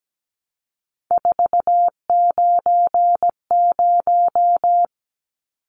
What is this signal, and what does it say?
Morse code, heard as a keyed tone, about 17 wpm, 700 Hz. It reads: 490
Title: Numbers - 3-Digit 17wpm